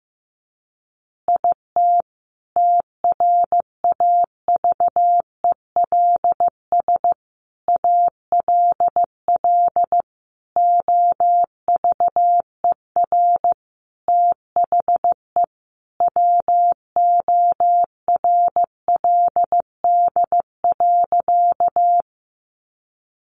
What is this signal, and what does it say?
Morse code, heard as a keyed tone, about 14 wpm, 700 Hz. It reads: IT TRAVELS ALL OVER THE WORLD.
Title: Additional Easy Phrases 15wpm